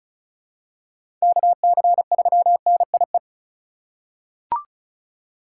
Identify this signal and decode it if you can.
Morse code, heard as a keyed tone, about 36 wpm, 700 Hz. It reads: KC3NIE
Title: K1USN SST QSO Practice - 35wpm